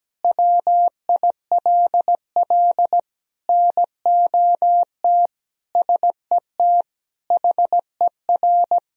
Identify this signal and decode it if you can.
Morse code, heard as a keyed tone, about 17 wpm, 700 Hz. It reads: WILL NOT SET HER
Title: Sentences from Top 1000 Words - Review Part 1 - 17wpm